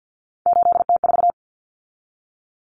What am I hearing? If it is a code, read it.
Morse code, heard as a keyed tone, about 50 wpm, 700 Hz. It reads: 8T4
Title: Common Cut Numbers - 3-Digit 50wpm